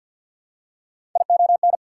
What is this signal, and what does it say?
Morse code, heard as a keyed tone, about 53 wpm, 690 Hz. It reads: ION
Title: Prefixes and Suffixes 50wpm